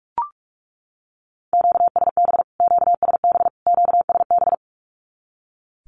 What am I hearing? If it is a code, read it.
Morse code, heard as a keyed tone, about 44 wpm, 700 Hz. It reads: QSB QSB QSB
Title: Q-Codes Repeated 3x 45wpm